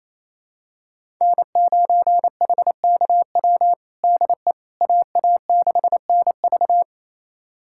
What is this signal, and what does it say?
Morse code, heard as a keyed tone, about 28 wpm, 700 Hz. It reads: N9HKW DE AA6NV
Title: USA Callsign DE Callsign - 5 Characters - 28wpm